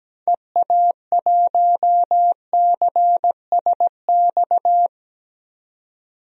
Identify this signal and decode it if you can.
Morse code, heard as a keyed tone, about 17 wpm, 700 Hz. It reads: EA1CSX
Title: DXCC-100 Call Signs - 6 Characters - 17wpm